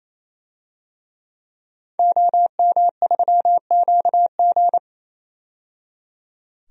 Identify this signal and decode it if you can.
Morse code, heard as a keyed tone, about 28 wpm, 700 Hz. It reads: OM3QG
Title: DXCC-100 Call Signs - 5 Characters - 28wpm